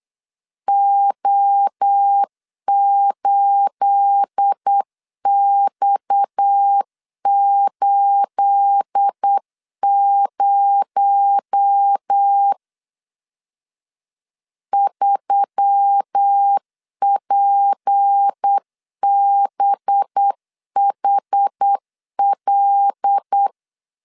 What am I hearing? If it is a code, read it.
Morse code, heard as a keyed tone, about 8 wpm, 800 Hz. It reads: O8X80 3PBHL
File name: LEZIONI / 9 28 8wpm 3512